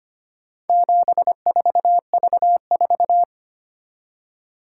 Morse code, heard as a keyed tone, about 25 wpm, 700 Hz. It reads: 74V4
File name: practice / 25wpm / Four Letter-Number Combinations - 25wpm